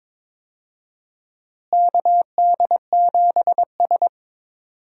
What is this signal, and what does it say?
Morse code, heard as a keyed tone, about 22 wpm, 700 Hz. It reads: KD7S
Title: USA Call Signs - 4 Characters - Repeated 3x - 22wpm